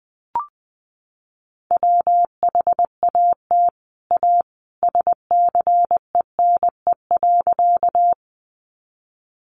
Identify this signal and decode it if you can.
Morse code, heard as a keyed tone, about 19 wpm, 700 Hz. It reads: WHAT A SCENE.
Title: Sentences from Top 1100 Words 20wpm